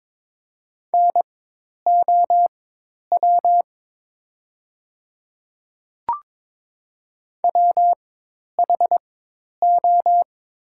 Morse code, heard as a keyed tone, about 22 wpm, spaced as for 10 wpm, 700 Hz. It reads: NOW WHO
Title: 4x Character Spacing - Top 100 Words - 22wpm